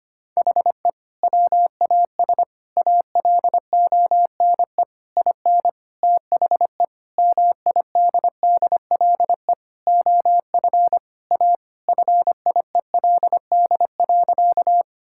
Morse code, heard as a keyed tone, about 25 wpm, 700 Hz. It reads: HE WAS ALONE IN THE MIDDLE OF A FIELD.
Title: Sentences from Top 1000 Words - Review Part 6 - 25wpm